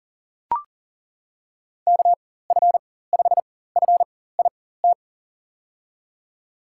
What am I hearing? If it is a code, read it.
Morse code, heard as a keyed tone, about 40 wpm, spaced as for 19 wpm, 700 Hz. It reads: KF5FIT
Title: USA Call Signs - 6 Characters - 4x Character Spacing - 40wpm